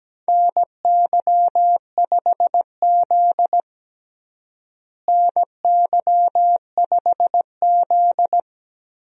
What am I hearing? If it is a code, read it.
Morse code, heard as a keyed tone, about 17 wpm, 700 Hz. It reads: NY5Z NY5Z
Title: USA Call Signs - 4 Characters - Repeated 3x - 17wpm